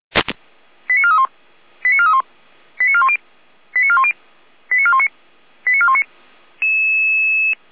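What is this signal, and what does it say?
Five-tone selective call (ZVEI), heard as quick runs of five tones, tones of about 70 ms. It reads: 89431 89431 89410 89410 89419 89419
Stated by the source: ZVEI five-tone selective call 89160 received on an SDR, fire brigade (Feuerwehr) Orlamünde.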